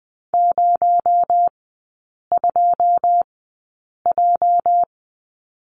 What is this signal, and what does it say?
Morse code, heard as a keyed tone, about 20 wpm, spaced as for 8 wpm, 700 Hz. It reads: 02J
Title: 218 Lesson 35e - Number 0 - Three Characters 20wpm